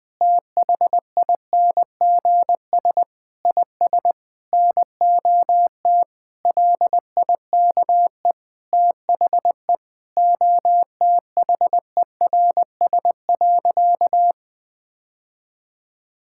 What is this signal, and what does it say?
Morse code, heard as a keyed tone, about 20 wpm, 700 Hz. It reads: THINGS IS NOT LIKE THE OTHERS.
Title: Sentences with 8-Word Phrases - 20wpm